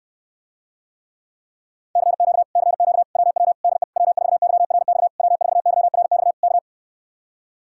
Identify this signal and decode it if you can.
Morse code, heard as a keyed tone, about 69 wpm, 700 Hz. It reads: CQ CQ WW DE W4YRX W4YRX K